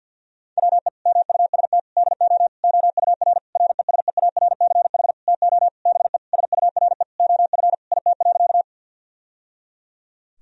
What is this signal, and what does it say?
Morse code, heard as a keyed tone, about 50 wpm, 700 Hz. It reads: WE MUST DO OUR RESEARCH TO BE SURE OF IT.
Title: Sentences from Top 1000 Words - Review Part 4 - 50wpm